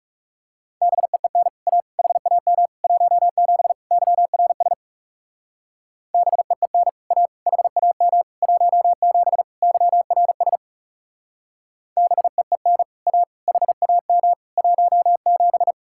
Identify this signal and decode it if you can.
Morse code, heard as a keyed tone, about 36 wpm, 700 Hz. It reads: BEEN A HAM 17 YRS BEEN A HAM 17 YRS BEEN A HAM 17
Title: Speed-Racing Common QSO Phrases - 30wpm